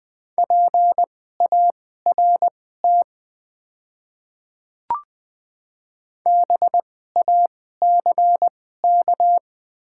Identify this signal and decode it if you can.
Morse code, heard as a keyed tone, about 20 wpm, spaced as for 15 wpm, 700 Hz. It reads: PART BACK
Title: 2x Character Spacing - Top 200 Words - 20wpm